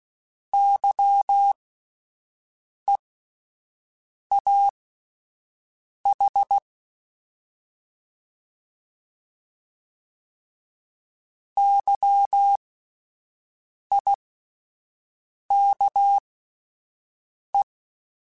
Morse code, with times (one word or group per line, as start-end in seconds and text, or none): 0.5-6.6 YEAH
11.6-17.6 YIKE